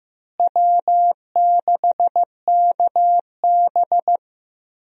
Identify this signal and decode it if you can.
Morse code, heard as a keyed tone, about 15 wpm, 700 Hz. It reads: W6KB
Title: USA Call Signs - 4 Characters - Repeated 3x - 15wpm